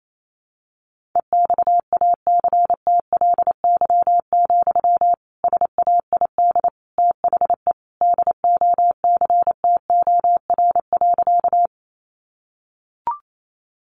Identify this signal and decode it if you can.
Morse code, heard as a keyed tone, about 28 wpm, 700 Hz. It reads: EXACTLY, SAID THE DOCTOR.
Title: Sentences from Top 1500 Words 28wpm